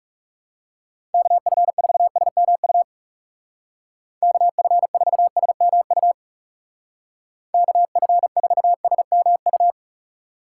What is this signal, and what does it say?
Morse code, heard as a keyed tone, about 38 wpm, 700 Hz. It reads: KF4SMU KF4SMU KF4SMU